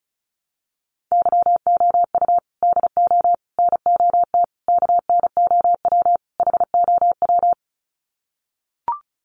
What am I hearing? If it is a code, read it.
Morse code, heard as a keyed tone, about 34 wpm, 700 Hz. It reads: YOU DO NOT KNOW HOW
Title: Sets of 5 Words Encore 35wpm